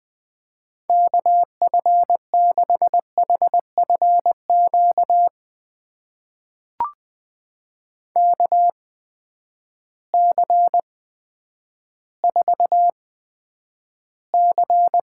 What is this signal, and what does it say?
Morse code, heard as a keyed tone, about 20 wpm, spaced as for 10 wpm, 700 Hz. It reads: KF6HFQ KC4C
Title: USA Call Signs - 6 Characters - 8x Character Spacing - 20wpm